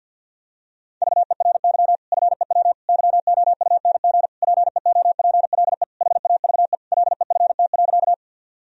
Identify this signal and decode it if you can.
Morse code, heard as a keyed tone, about 46 wpm, 700 Hz. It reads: VERY FEW YOUNG PEOPLE HAVE LEFT.
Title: Sentences from Top 1000 Words - Review Part 1 - 50wpm